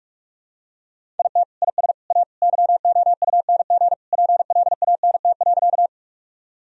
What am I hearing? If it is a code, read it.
Morse code, heard as a keyed tone, about 46 wpm, 700 Hz. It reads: IT IS A YOUNG PLANT.
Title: Sentences from Top 1000 Words - Review Part 1 - 45wpm